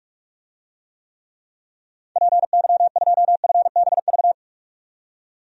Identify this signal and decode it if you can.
Morse code, heard as a keyed tone, about 42 wpm, 700 Hz. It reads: PY2FBV
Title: DXCC-100 Call Signs - 6 Characters - 45wpm